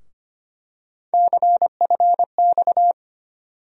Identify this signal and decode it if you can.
Morse code, heard as a keyed tone, about 25 wpm, 700 Hz. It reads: CFX
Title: USA Call Sign Postfixes - 3 Letters 25wpm